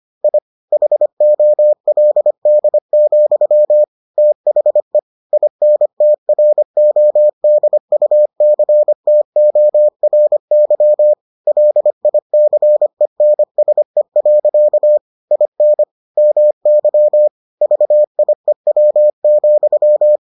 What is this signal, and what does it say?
Morse code, heard as a keyed tone, about 25 wpm, 600 Hz. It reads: I HOLD, THE INTRODUCTORY LICENSE. IN MY VIEW,